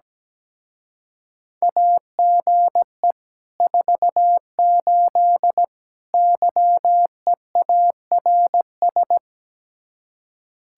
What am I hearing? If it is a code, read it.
Morse code, heard as a keyed tone, about 17 wpm, 700 Hz. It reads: AGE 48 YEARS
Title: Speed-Racing Common QSO Phrases Encore - 17wpm